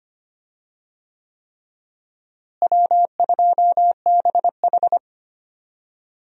Morse code, heard as a keyed tone, about 25 wpm, 700 Hz. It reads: W2BH
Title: USA Call Signs - 4 Characters - Repeated 3x - 25wpm